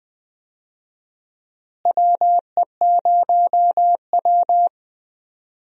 Morse code, heard as a keyed tone, about 21 wpm, 700 Hz. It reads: WE0W
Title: USA Call Signs - 4 Characters - Repeated 3x - 20wpm